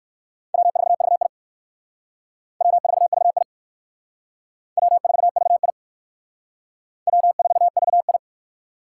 Morse code, heard as a keyed tone, about 57 wpm, 700 Hz. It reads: W4VI W4VI W4VI W4VI